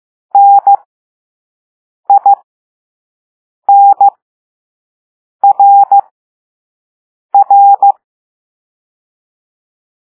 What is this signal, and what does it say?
Morse code, heard as a keyed tone, about 15 wpm, spaced as for 5 wpm, 800 Hz. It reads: NINRR